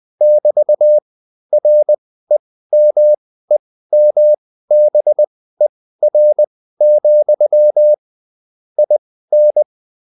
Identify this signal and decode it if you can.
Morse code, heard as a keyed tone, about 20 wpm, spaced as for 14 wpm, 600 Hz. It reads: <BT>REMEMBER, IN